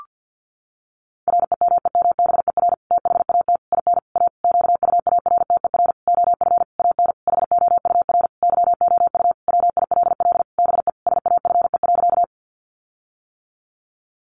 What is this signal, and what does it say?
Morse code, heard as a keyed tone, about 50 wpm, 700 Hz. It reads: REMEMBER THAT IN A QUARTER OF AN HOUR YOU WILL BE SAFE.
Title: Sentences from Top 1300 Words 50wpm